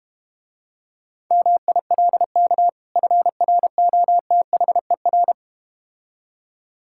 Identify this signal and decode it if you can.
Morse code, heard as a keyed tone, about 32 wpm, 700 Hz. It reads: MILK FROTHER